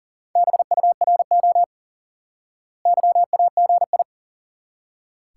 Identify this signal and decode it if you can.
Morse code, heard as a keyed tone, about 40 wpm, 700 Hz. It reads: BURO YAGI
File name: IQE Course 3 - Lesson 89 - YAGI - Two Elements - 40wpm